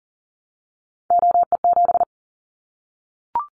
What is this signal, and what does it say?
Morse code, heard as a keyed tone, about 41 wpm, 700 Hz. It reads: OE7
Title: Callsign Prefixes for Top 50 DXCC - 40wpm